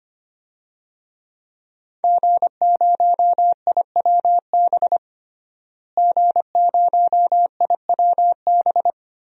Morse code, heard as a keyed tone, about 25 wpm, 700 Hz. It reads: G0IWB G0IWB
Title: DXCC-100 Call Signs - 5 Characters - Repeated 2x - 25wpm